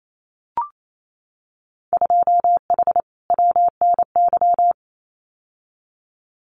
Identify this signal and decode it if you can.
Morse code, heard as a keyed tone, about 28 wpm, 700 Hz. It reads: 2H WNY
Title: Winter Field Day Exchanges Encore 28wpm